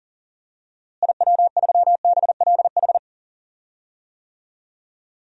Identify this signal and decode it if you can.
Morse code, heard as a keyed tone, about 40 wpm, 700 Hz. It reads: IW3BLH
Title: DXCC-100 Call Signs - 6 Characters - Repeated 3x - 40wpm